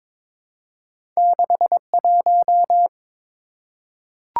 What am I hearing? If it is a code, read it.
Morse code, heard as a keyed tone, about 22 wpm, 700 Hz. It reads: 61
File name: practice / 22wpm / Two Letter-Number Combinations - 22wpm